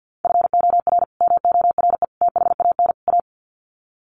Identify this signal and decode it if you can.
Morse code, heard as a keyed tone, about 50 wpm, 700 Hz. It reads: FOR MORE THAN A
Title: Sets of 4 Words 50wpm